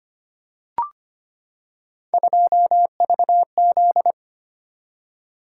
Morse code, heard as a keyed tone, about 25 wpm, 700 Hz. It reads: 2VZ
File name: Three Letter-Number Combinations - 25wpm